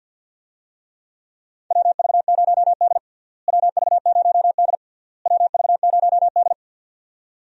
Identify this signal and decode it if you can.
Morse code, heard as a keyed tone, about 53 wpm, 700 Hz. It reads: WV0D WV0D WV0D
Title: USA Call Signs - 4 Characters - Repeated 3x - 50wpm